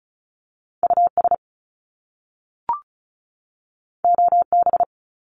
Morse code, heard as a keyed tone, about 36 wpm, 700 Hz. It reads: US OB
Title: Top Two Letter Combinations in Words 35wpm